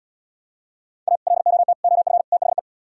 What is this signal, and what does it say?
Morse code, heard as a keyed tone, about 77 wpm, 700 Hz. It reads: A LOT OF THE